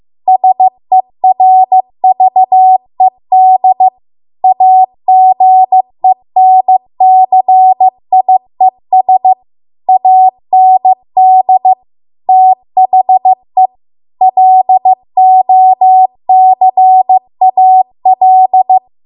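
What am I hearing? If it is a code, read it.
Morse code, heard as a keyed tone, about 15 wpm, 750 Hz. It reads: SERVED AGENCIES AND THE LOCAL
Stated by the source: Morse code practice at 15 wpm from the national amateur radio association's code archive.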